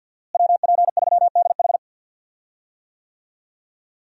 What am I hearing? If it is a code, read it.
Morse code, heard as a keyed tone, about 51 wpm, 700 Hz. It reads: WP3DH
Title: USA Call Signs - 5 Characters - Repeated 2x - 50wpm